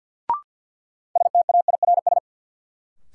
Morse code, heard as a keyed tone, about 50 wpm, 700 Hz. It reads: STAIRS
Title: Sets of 5 Words Encore - New Word Drill - 50wpm